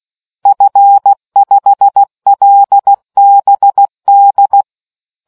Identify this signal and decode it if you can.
Morse code, heard as a keyed tone, about 16 wpm, 800 Hz. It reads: F5LBD